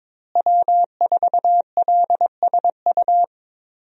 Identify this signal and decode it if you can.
Morse code, heard as a keyed tone, about 22 wpm, 700 Hz. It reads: W4LSU
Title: USA Call Signs - 5 Characters - Repeated 3x - 22wpm